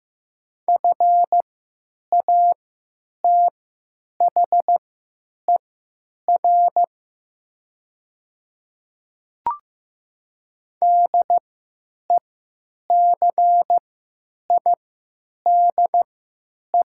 Morse code, heard as a keyed tone, about 15 wpm, spaced as for 9 wpm, 700 Hz. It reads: FATHER DECIDE
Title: 3x Character Spacing - Top 500 Words - 15wpm